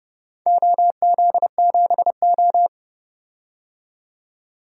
Morse code, heard as a keyed tone, about 30 wpm, 700 Hz. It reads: OZ7O
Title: DXCC-100 Call Signs - 4 Characters - 30wpm